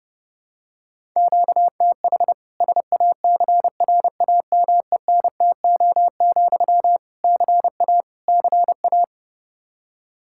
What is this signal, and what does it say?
Morse code, heard as a keyed tone, about 30 wpm, 700 Hz. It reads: QTH SACRAMENTO, CA CA
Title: K1USN SST QSO Practice 30wpm